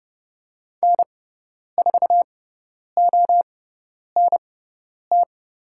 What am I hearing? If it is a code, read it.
Morse code, heard as a keyed tone, about 31 wpm, spaced as for 10 wpm, 700 Hz. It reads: N4ONT
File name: Lesson 18g - Letter m - Call Signs 30wpm